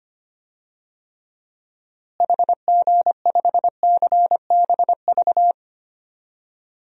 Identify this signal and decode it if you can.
Morse code, heard as a keyed tone, about 25 wpm, 700 Hz. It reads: HG5CBV
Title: DXCC-100 Call Signs - 6 Characters - Repeated 2x - 25wpm